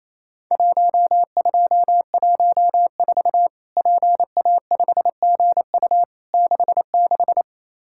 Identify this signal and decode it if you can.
Morse code, heard as a keyed tone, about 28 wpm, 700 Hz. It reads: 1214 PA5GU 66
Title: Worked All Europe DX Contest - QTC Exchange - 28wpm